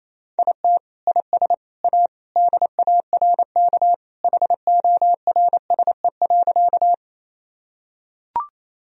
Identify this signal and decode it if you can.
Morse code, heard as a keyed tone, about 28 wpm, 700 Hz. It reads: IT IS A DARK HORSE.
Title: Sentences from Top 1000 Words - Review Part 1 - 28wpm